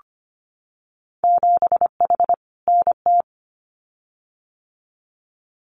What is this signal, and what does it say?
Morse code, heard as a keyed tone, about 25 wpm, 700 Hz. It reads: 7H NT
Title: Winter Field Day Exchanges Encore 25wpm